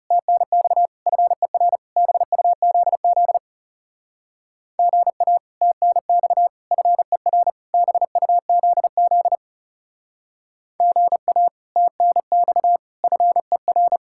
Morse code, heard as a keyed tone, about 36 wpm, 700 Hz. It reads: TNX FER BUZZ GA TNX FER BUZZ GA TNX FER
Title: Speed-Racing Common QSO Phrases - 30wpm